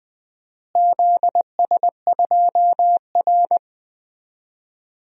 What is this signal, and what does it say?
Morse code, heard as a keyed tone, about 21 wpm, 700 Hz. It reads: ZS2R
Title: DXCC-100 Call Signs - 4 Characters - Repeated 3x - 20wpm